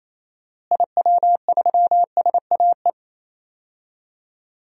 Morse code, heard as a keyed tone, about 28 wpm, 700 Hz. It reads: IW3SAE
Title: DXCC-100 Call Signs - 6 Characters - Repeated 3x - 28wpm